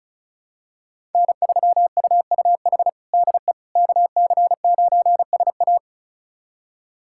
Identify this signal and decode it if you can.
Morse code, heard as a keyed tone, about 36 wpm, 700 Hz. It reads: N3UUH DE KC9SA
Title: USA Callsign DE Callsign - 5 Characters - 35wpm